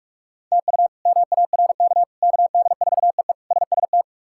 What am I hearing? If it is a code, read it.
Morse code, heard as a keyed tone, about 41 wpm, 700 Hz. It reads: TU MARK KD4EE SST
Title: K1USN SST QSO Practice - 45wpm